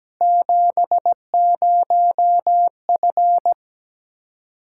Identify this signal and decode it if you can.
Morse code, heard as a keyed tone, about 17 wpm, 700 Hz. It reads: 70F
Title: Practice QSO - Exchange 2 - Info - 17wpm